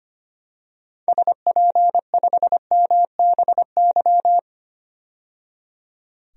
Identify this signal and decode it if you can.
Morse code, heard as a keyed tone, about 25 wpm, 700 Hz. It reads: SP5MBY
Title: DXCC-100 Call Signs - 6 Characters - 25wpm